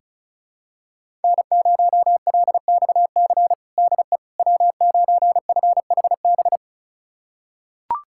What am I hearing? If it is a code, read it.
Morse code, heard as a keyed tone, about 36 wpm, 700 Hz. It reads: N0LXC DE W9FHB
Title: USA Callsign DE Callsign - 5 Characters - 35wpm